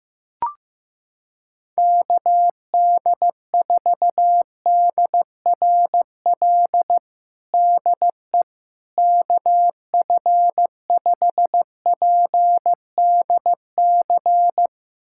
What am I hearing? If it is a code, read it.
Morse code, heard as a keyed tone, about 15 wpm, 700 Hz. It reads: KD4DRL DE KF5PDC
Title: USA Callsign DE Callsign - 6 Characters - 15wpm